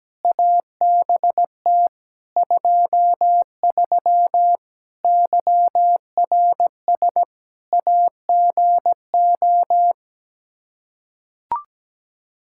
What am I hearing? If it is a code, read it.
Morse code, heard as a keyed tone, about 17 wpm, 700 Hz. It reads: ABT 23 YRS AGO
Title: Practice QSO - Exchange 3 - More Info - 17wpm